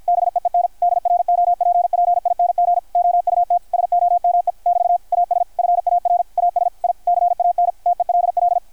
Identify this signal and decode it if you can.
Morse code, heard as a keyed tone, about 53 wpm, 720 Hz, with radio noise. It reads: BEEN DROPPING OUT SOME <BT> AS FAR AS I CAN TELL
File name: archived / CRSnet